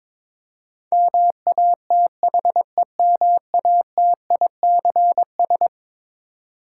Morse code, heard as a keyed tone, about 21 wpm, 700 Hz. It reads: MATHEMATICS
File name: practice / 22wpm / Eleven Letter Words - 22wpm